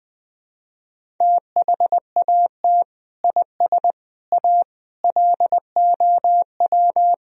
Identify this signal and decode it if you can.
Morse code, heard as a keyed tone, about 21 wpm, 700 Hz. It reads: THAT IS A LOW
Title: Sentences from Top 1000 Words - Review Part 1 - 20wpm